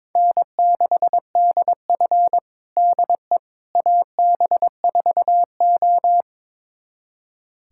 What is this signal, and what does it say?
Morse code, heard as a keyed tone, about 21 wpm, 700 Hz. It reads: N6DF DE AB4O
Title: USA Callsign DE Callsign - 4 Characters - 22wpm